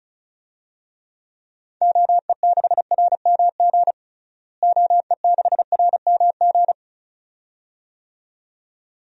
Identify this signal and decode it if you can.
Morse code, heard as a keyed tone, about 34 wpm, 700 Hz. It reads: OE6RMG OE6RMG
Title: DXCC-100 Call Signs - 6 Characters - Repeated 2x - 35wpm